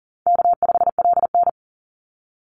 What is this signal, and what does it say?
Morse code, heard as a keyed tone, about 40 wpm, 700 Hz. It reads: K5LN
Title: Call Signs - 40wpm